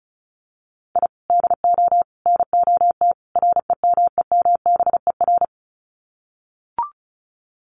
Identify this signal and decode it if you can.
Morse code, heard as a keyed tone, about 34 wpm, 700 Hz. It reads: I DO NOT REMEMBER
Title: Sets of 4 Words 35wpm